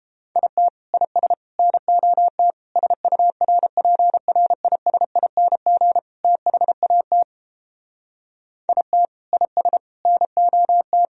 Morse code, heard as a keyed tone, about 29 wpm, 700 Hz. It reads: IT IS NOT SURPRISING THAT IT IS NOT